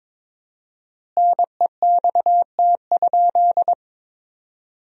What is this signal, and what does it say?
Morse code, heard as a keyed tone, about 21 wpm, 700 Hz. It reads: NEXT?
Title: 4x Word-Spacing - Sentences from Top 300 Words - 22wpm